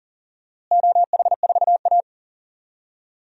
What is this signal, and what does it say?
Morse code, heard as a keyed tone, about 40 wpm, 700 Hz. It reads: OH4A
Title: DXCC-100 Call Signs - 4 Characters - Repeated 3x - 40wpm